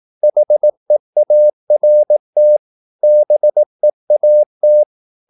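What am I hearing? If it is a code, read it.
Morse code, heard as a keyed tone, about 18 wpm, 600 Hz. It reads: HEART BEAT
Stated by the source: Morse practice at 18 wpm, English text, from CW med Gnister.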